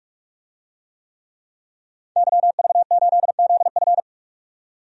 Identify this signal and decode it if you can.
Morse code, heard as a keyed tone, about 41 wpm, 700 Hz. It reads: YV8ZF
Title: DXCC-100 Call Signs - 5 Characters - 45wpm